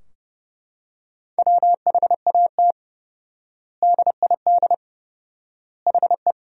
Morse code, heard as a keyed tone, about 32 wpm, 700 Hz. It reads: WHAT DID HE
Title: 4x Word-Spacing - Sentences from Top 300 Words - 30wpm